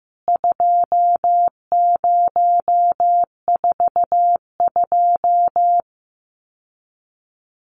Morse code, heard as a keyed tone, about 15 wpm, 700 Hz. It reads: 2042
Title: CWT Exchange 15wpm